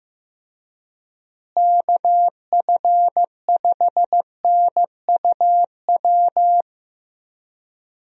Morse code, heard as a keyed tone, about 16 wpm, 700 Hz. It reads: KF5NUW
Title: USA Call Signs - 6 Characters - Repeated 2x - 15wpm